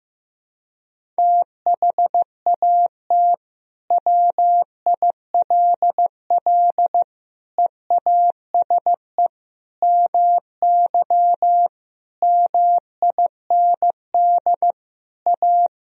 Morse code, heard as a keyed tone, about 15 wpm, 700 Hz. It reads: THAT WILL EASE MY MIND A
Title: Sentences from Top 1000 Words - Review Part 1 - 15wpm